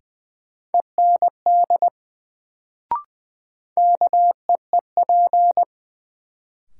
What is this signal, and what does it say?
Morse code, heard as a keyed tone, about 20 wpm, 700 Hz. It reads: END KEEP
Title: Sets of 2 Words - New Word Drill - 20wpm